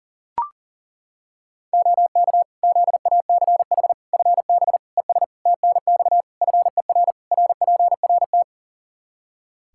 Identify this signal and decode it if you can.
Morse code, heard as a keyed tone, about 40 wpm, 700 Hz. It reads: OK ZACH FB ES TNX FER RPRT
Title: K1USN SST QSO Practice 40wpm